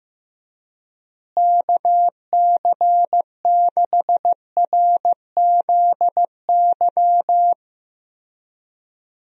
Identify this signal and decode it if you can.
Morse code, heard as a keyed tone, about 15 wpm, 700 Hz. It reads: KC6RZY
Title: USA Call Signs - 6 Characters - Repeated 3x - 15wpm